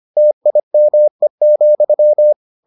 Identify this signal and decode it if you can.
Morse code, heard as a keyed tone, about 25 wpm, 600 Hz. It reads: TIME,